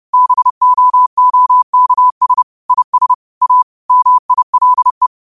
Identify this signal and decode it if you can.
Morse code, heard as a keyed tone, about 30 wpm, 1000 Hz. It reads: BOOKS IS A MILE